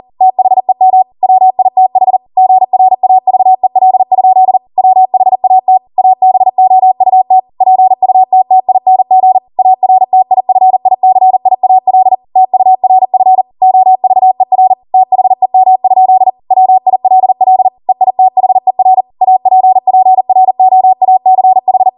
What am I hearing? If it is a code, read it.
Morse code, heard as a keyed tone, about 40 wpm, 750 Hz. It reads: THEM WITH GRAVEL? WHAT ABOUT PUTTING ARTIFICIAL TURF OVER THEM? WILL EITHER APPROACH
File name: Morse / 40WPM